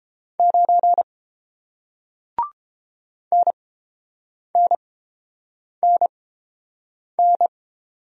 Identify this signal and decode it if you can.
Morse code, heard as a keyed tone, about 30 wpm, spaced as for 7 wpm, 700 Hz. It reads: 9 NNNN